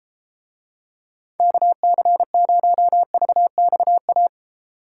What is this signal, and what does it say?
Morse code, heard as a keyed tone, about 32 wpm, 700 Hz. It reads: KC0VXA